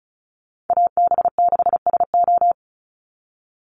Morse code, heard as a keyed tone, about 34 wpm, 700 Hz. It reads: AB6SO
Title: NA SOTA Chaser Callsigns 35wpm